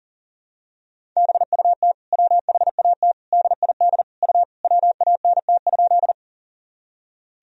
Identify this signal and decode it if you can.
Morse code, heard as a keyed tone, about 40 wpm, 700 Hz. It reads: BUT WHAT DID U WANT?
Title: Sentences from Top 200 Words and Abbreviations 40wpm